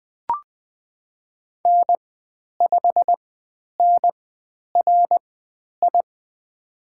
Morse code, heard as a keyed tone, about 20 wpm, spaced as for 10 wpm, 700 Hz. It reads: N5NRI
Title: 095 Lesson 16g - Letter c - Call Signs 20wpm